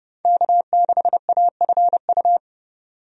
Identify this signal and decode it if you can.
Morse code, heard as a keyed tone, about 30 wpm, 700 Hz. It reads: K6AFU
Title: USA Call Signs - 5 Characters - 30wpm